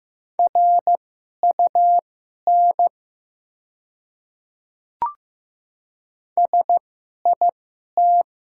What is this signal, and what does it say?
Morse code, heard as a keyed tone, about 15 wpm, spaced as for 11 wpm, 700 Hz. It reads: RUN SIT
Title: 2x Character Spacing - Top 500 Words - 15wpm